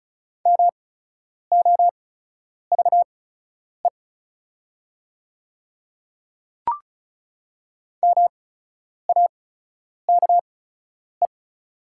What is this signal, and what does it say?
Morse code, heard as a keyed tone, about 36 wpm, spaced as for 10 wpm, 700 Hz. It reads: MOVE MAKE
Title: 8x Character Spacing - Top 200 Words - 35wpm